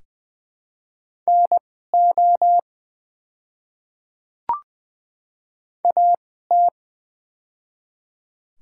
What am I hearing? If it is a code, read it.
Morse code, heard as a keyed tone, about 20 wpm, spaced as for 14 wpm, 700 Hz. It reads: NO AT
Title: 2x Character Spacing - Top 100 Words - 20wpm